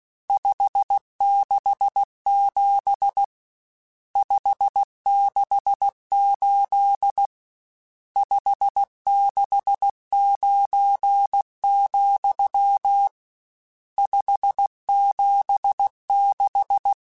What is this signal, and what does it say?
Morse code, heard as a keyed tone, about 16 wpm, 800 Hz. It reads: 567 568 569, 576
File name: RSTs / RST-mp3s / Various RST @15